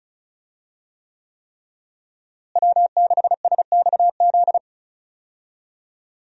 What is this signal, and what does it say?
Morse code, heard as a keyed tone, about 36 wpm, 700 Hz. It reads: W6SXZ
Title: USA Call Signs - 5 Characters - Repeated 3x - 35wpm